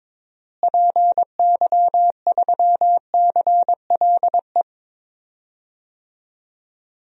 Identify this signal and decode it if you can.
Morse code, heard as a keyed tone, about 22 wpm, 700 Hz. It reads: PY3CLE
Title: DXCC-100 Call Signs - 6 Characters - Repeated 2x - 22wpm